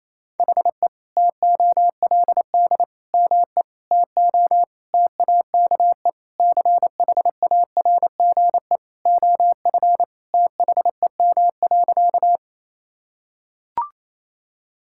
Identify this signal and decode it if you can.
Morse code, heard as a keyed tone, about 28 wpm, 700 Hz. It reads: HE TOLD ME TO TAKE CHARGE OF THEM.
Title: Sentences from Top 1000 Words - Review Part 6 - 28wpm